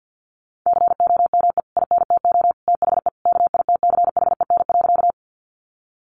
Morse code, heard as a keyed tone, about 50 wpm, 700 Hz. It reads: COME INTO THE KITCHEN.
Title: Sentences from Top 1400 Words 50wpm